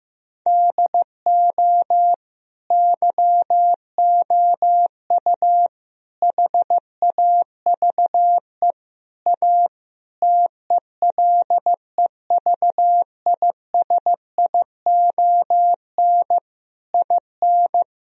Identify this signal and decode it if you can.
Morse code, heard as a keyed tone, about 15 wpm, 700 Hz. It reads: DO YOU HAVE A TELEVISION IN
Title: Sentences from Top 1000 Words - Review Part 4 - 15wpm